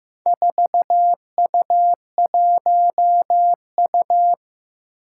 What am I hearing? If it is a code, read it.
Morse code, heard as a keyed tone, about 15 wpm, 700 Hz. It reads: 4U1U
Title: Four Challenging Letter-Number Combinations 15wpm